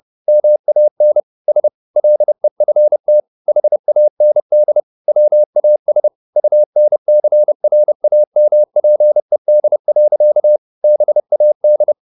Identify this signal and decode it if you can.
Morse code, heard as a keyed tone, about 30 wpm, 600 Hz. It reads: MAN S LEFT HAND WAS UNCRAMPED. BAD
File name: the old man and the sea / the old man and the sea 1 30wpm